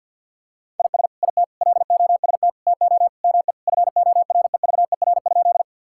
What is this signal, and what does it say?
Morse code, heard as a keyed tone, about 50 wpm, 700 Hz. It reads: IS IT LOST TO ME FOREVER?
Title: Sentences from Top 1600 Words 50wpm